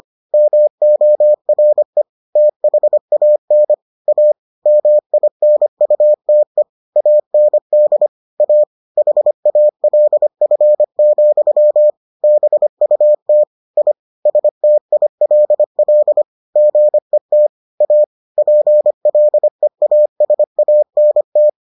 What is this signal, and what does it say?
Morse code, heard as a keyed tone, about 25 wpm, 600 Hz. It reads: MORE THAN A MINUTE AND A HALF, BUT I STILL GET A PLEASANT